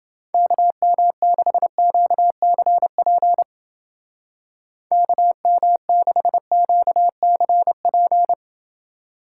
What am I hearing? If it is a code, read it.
Morse code, heard as a keyed tone, about 29 wpm, 700 Hz. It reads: KM6QCP KM6QCP